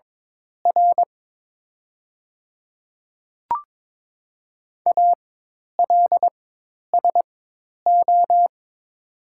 Morse code, heard as a keyed tone, about 22 wpm, spaced as for 10 wpm, 700 Hz. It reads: R ALSO
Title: 4x Character Spacing - Top 200 Words - 22wpm